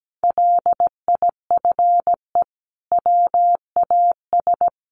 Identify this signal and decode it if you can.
Morse code, heard as a keyed tone, about 17 wpm, 700 Hz. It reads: LIFE WAS
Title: Sentences from Top 1200 Words 17wpm